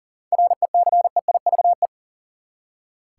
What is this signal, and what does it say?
Morse code, heard as a keyed tone, about 40 wpm, 700 Hz. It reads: RECEIVE
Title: Seven Letter Words - 40wpm